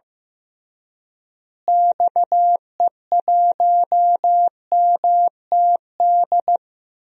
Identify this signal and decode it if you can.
Morse code, heard as a keyed tone, about 15 wpm, 700 Hz. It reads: XE1MTD
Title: DXCC-100 Call Signs - 6 Characters - Repeated 2x - 15wpm